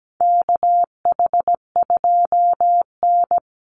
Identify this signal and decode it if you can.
Morse code, heard as a keyed tone, about 17 wpm, 700 Hz. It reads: KH2N
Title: Call Signs Repeated 3x - 17wpm